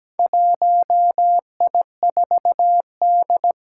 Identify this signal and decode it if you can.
Morse code, heard as a keyed tone, about 17 wpm, 700 Hz. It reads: 1I4D
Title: Four Challenging Letter-Number Combinations 17wpm